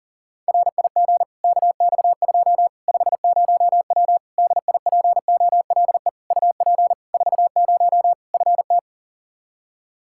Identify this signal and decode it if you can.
Morse code, heard as a keyed tone, about 40 wpm, 700 Hz. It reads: RIG KX2 50W DIPOLE UP 40 FT
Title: Practice QSO - Exchange 2 - Info - 40wpm